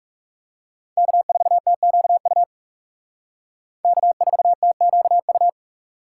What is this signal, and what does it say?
Morse code, heard as a keyed tone, about 36 wpm, 700 Hz. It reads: K4TQU K4TQU
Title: Speed Racing - USA 1x3 Call Signs 30wpm